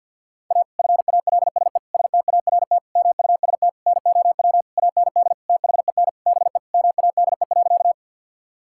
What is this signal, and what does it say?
Morse code, heard as a keyed tone, about 52 wpm, 700 Hz. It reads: A FALSE START MUST NOW AND THEN BE MADE.
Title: Sentences from Top 1600 Words 50wpm